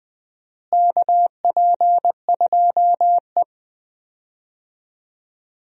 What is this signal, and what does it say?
Morse code, heard as a keyed tone, about 21 wpm, 700 Hz. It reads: KP2E
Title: DXCC-100 Call Signs - 4 Characters - Repeated 3x - 20wpm